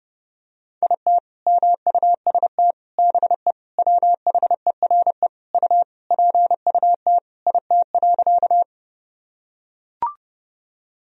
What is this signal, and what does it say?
Morse code, heard as a keyed tone, about 30 wpm, 700 Hz. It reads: IT MUST BE WHERE U PUT IT.
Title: Sentences from Top 200 Words and Abbreviations 30wpm